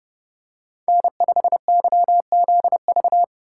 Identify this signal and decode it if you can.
Morse code, heard as a keyed tone, about 28 wpm, 700 Hz. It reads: N5YZV